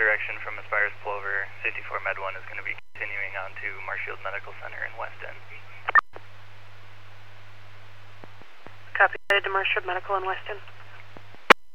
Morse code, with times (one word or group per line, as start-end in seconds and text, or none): none